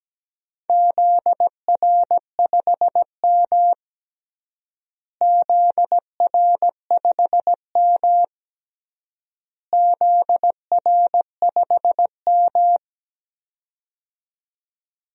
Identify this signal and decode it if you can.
Morse code, heard as a keyed tone, about 17 wpm, 700 Hz. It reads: ZR5M ZR5M ZR5M
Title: DXCC-100 Call Signs - 4 Characters - Repeated 3x - 17wpm